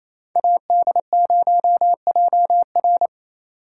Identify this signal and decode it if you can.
Morse code, heard as a keyed tone, about 32 wpm, 700 Hz. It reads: AD0JR
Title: Speed Racing - USA 2x2 Call Signs 28wpm